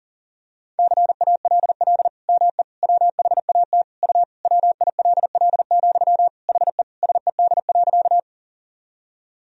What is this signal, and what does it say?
Morse code, heard as a keyed tone, about 40 wpm, 700 Hz. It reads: CALL ME WHAT U WILL, HE SED.
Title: Sentences from Top 200 Words and Abbreviations 40wpm